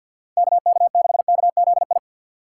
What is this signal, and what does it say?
Morse code, heard as a keyed tone, about 51 wpm, 690 Hz. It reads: KK6KZI
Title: USA Call Signs - 6 Characters - 3x Character Spacing - 50wpm